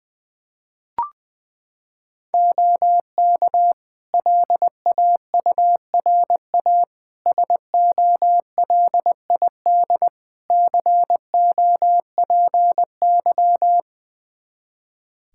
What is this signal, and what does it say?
Morse code, heard as a keyed tone, about 20 wpm, 700 Hz. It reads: OK LAURA SOLID COPY
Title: K1USN SST QSO Practice 20wpm